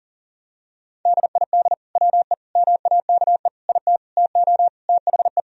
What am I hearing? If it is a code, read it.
Morse code, heard as a keyed tone, about 40 wpm, 700 Hz. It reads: DID WE MAKE IT TO THE
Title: Sentences from Top 1000 Words - Review Part 3 - 40wpm